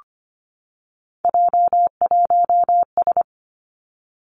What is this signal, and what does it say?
Morse code, heard as a keyed tone, about 25 wpm, 700 Hz. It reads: J1S
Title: Three Challenging Letter-Number Combinations 25wpm